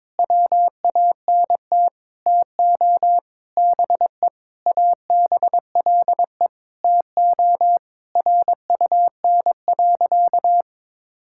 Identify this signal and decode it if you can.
Morse code, heard as a keyed tone, about 21 wpm, 700 Hz. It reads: WANT TO BE ABLE TO RUN.
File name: practice / 22wpm / Sentences with 3-Word Phrases - 22wpm